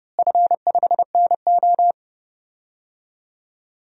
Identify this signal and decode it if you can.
Morse code, heard as a keyed tone, about 30 wpm, 700 Hz. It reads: F5NO
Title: DXCC-100 Call Signs - 4 Characters - Repeated 2x - 30wpm